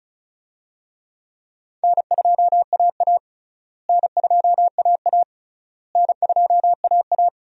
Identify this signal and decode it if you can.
Morse code, heard as a keyed tone, about 34 wpm, 700 Hz. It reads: N2AA N2AA N2AA
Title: USA Call Signs - 4 Characters - Repeated 3x - 35wpm